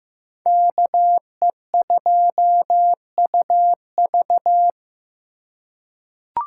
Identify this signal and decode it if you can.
Morse code, heard as a keyed tone, about 15 wpm, 700 Hz. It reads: KE2UV
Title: USA Call Signs - 5 Characters - 3x Character Spacing - 15wpm